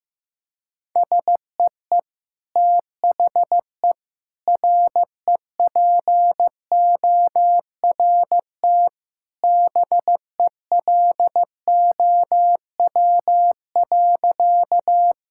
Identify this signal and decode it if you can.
Morse code, heard as a keyed tone, about 16 wpm, 700 Hz. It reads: SEE THE REPORT BELOW.
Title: Sentences from Top 1000 Words - Review Part 6 - 15wpm